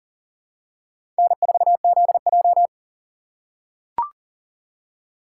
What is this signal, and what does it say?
Morse code, heard as a keyed tone, about 39 wpm, 700 Hz. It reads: N4ZJ
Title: USA Call Signs - 6x Character Spacing - 40wpm